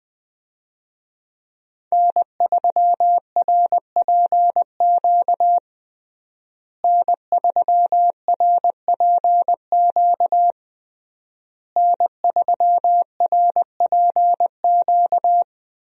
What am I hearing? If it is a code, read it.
Morse code, heard as a keyed tone, about 21 wpm, 700 Hz. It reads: N3RPQ N3RPQ N3RPQ
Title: USA Call Signs - 5 Characters - Repeated 3x - 20wpm